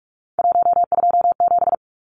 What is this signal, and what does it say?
Morse code, heard as a keyed tone, about 44 wpm, 700 Hz. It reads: 127
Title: Numbers - 3-Digit 45wpm